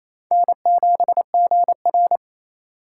Morse code, heard as a keyed tone, about 28 wpm, 700 Hz. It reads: N7GR
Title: USA Call Signs - 3x Character Spacing - 28wpm